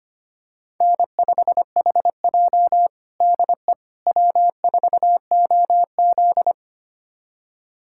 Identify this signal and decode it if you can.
Morse code, heard as a keyed tone, about 25 wpm, 700 Hz. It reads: N5HJ DE W4OZ
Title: USA Callsign DE Callsign - 4 Characters - 25wpm